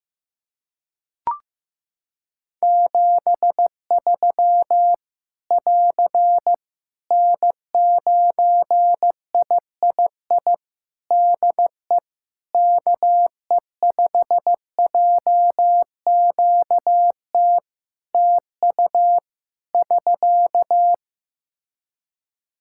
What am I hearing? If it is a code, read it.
Morse code, heard as a keyed tone, about 15 wpm, 700 Hz. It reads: 73 <AR> N9III DE KE5JQT TU <SK>
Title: Practice QSO - Exchange 4 - Ending - 15wpm